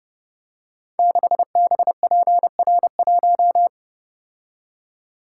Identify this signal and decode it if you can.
Morse code, heard as a keyed tone, about 30 wpm, 700 Hz. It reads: 6BPR1
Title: Code Groups - 5 Letter:Numbers - 30wpm